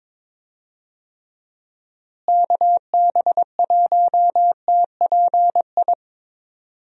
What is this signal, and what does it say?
Morse code, heard as a keyed tone, about 23 wpm, 700 Hz. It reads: KB1TPI
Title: USA Call Signs - 6 Characters - Repeated 3x - 22wpm